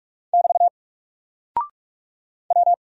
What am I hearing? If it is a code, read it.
Morse code, heard as a keyed tone, about 44 wpm, 700 Hz. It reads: <BT> W
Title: Single Character (Letter:Number:Punctuation:Prosign) Alt 45wpm